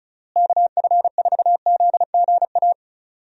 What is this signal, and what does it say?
Morse code, heard as a keyed tone, about 36 wpm, 700 Hz. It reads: KF4ZGA
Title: USA Call Signs - 6 Characters - Repeated 3x - 35wpm